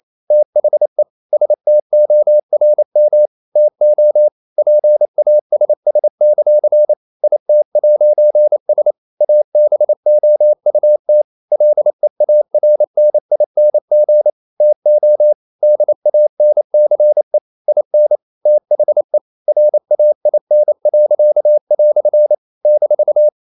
Morse code, heard as a keyed tone, about 28 wpm, 600 Hz. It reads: THE STORM TO PASS; IT'S ABOUT LEARNING TO DANCE IN THE RAIN." -
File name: CW / Practice / EnglishQuotations0128